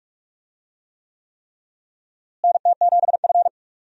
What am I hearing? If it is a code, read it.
Morse code, heard as a keyed tone, about 42 wpm, 700 Hz. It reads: NT7F
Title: USA Call Signs - 4 Characters - Repeated 3x - 45wpm